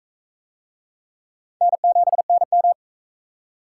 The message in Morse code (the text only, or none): N7NM